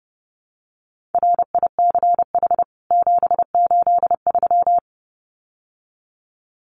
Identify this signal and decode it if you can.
Morse code, heard as a keyed tone, about 28 wpm, 700 Hz. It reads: RICH 783
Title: CWT Exchange 30wpm